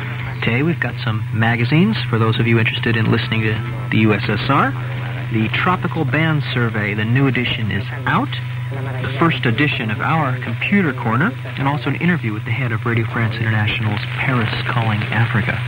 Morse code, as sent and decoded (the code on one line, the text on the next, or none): none